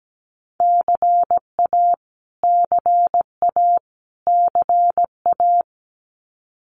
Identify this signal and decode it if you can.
Morse code, heard as a keyed tone, about 17 wpm, 700 Hz. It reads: CA CA CA
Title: US States Repeated 3x 17wpm